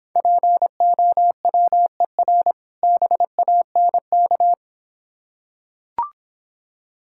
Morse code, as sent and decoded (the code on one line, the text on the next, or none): .--. --- .-- . .-. / -... .- -. -.-
POWER BANK